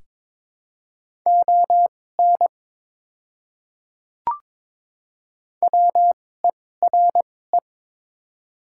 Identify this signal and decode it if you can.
Morse code, heard as a keyed tone, about 22 wpm, spaced as for 16 wpm, 690 Hz. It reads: ON WERE
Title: 2x Character Spacing - Top 100 Words - 22wpm